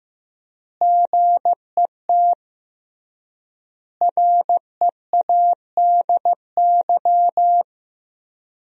Morse code, heard as a keyed tone, about 16 wpm, 700 Hz. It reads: GET READY
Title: 3x Word-Spacing - Sentences from Top 400 Words - 15wpm